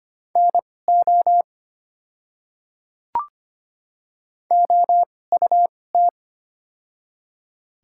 Morse code, heard as a keyed tone, about 25 wpm, spaced as for 18 wpm, 690 Hz. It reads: NO OUT
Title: 2x Character Spacing - Top 100 Words - 25wpm